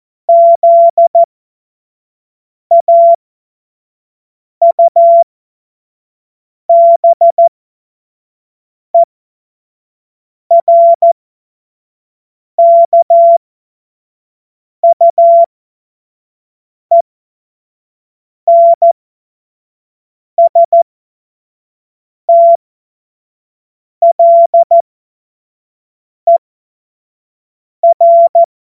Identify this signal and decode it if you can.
Morse code, heard as a keyed tone, about 14 wpm, spaced as for 5 wpm, 680 Hz. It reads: ZAUBERKUENSTLER
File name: Morsekurs / Witze / Witz 28.November 2023 25 F70